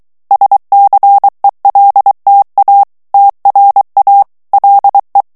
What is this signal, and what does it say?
Morse code, heard as a keyed tone, about 23 wpm, 800 Hz. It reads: SCELTA TRA LE
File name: ITALIEN texte clair 2 vitesse 22 mots minute